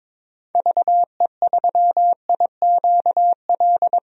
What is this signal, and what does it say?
Morse code, heard as a keyed tone, about 22 wpm, 700 Hz. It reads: VE3IQL
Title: DXCC-100 Call Signs - 6 Characters - 22wpm